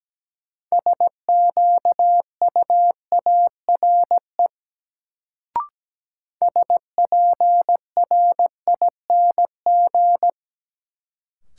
Morse code, heard as a keyed tone, about 17 wpm, 700 Hz. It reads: SQUARE SPRING
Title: Sentences from Top 1200 Words - New Word Drill Alt - 17wpm